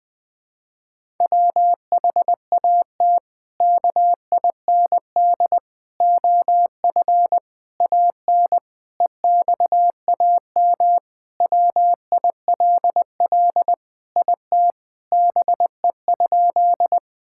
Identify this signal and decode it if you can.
Morse code, heard as a keyed tone, about 20 wpm, 700 Hz. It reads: WHAT KIND OF AN EXAM WILL IT BE?
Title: Sentences from Top 1000 Words - Review Part 5 - 20wpm